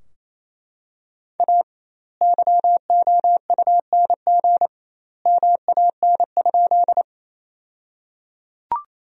Morse code, heard as a keyed tone, about 28 wpm, 700 Hz. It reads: A YOUNG MAN?
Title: 2x Word-Spacing - Sentences from Top 400 Words - 28wpm